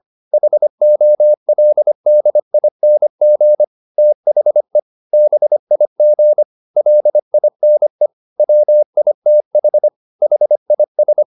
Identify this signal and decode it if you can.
Morse code, heard as a keyed tone, about 25 wpm, 600 Hz. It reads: HOLDING THE BIG LINE WITH HIS